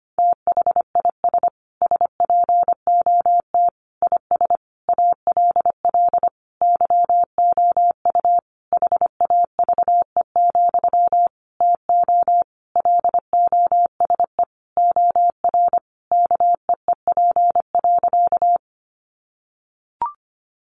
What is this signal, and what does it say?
Morse code, heard as a keyed tone, about 25 wpm, 700 Hz. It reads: THIS SPOT IS ALL YOU HAVE, TO LOSE OR KEEP.
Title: Sentences from Top 1100 Words 25wpm